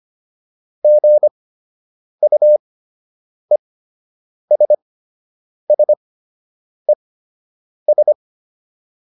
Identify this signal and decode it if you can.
Morse code, heard as a keyed tone, about 25 wpm, spaced as for 8 wpm, 600 Hz. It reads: GUESSES